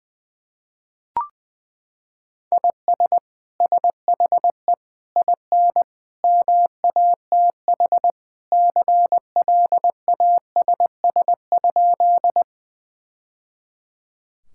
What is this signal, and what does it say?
Morse code, heard as a keyed tone, about 21 wpm, 700 Hz. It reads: IS SHE IN MATH CLASS?
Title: Sentences from Top 1000 Words - Review Part 6 - 20wpm